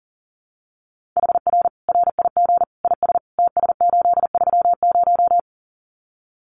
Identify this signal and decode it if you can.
Morse code, heard as a keyed tone, about 40 wpm, 700 Hz. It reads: HR RIG IS TS830
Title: Common QSO Phrases 40wpm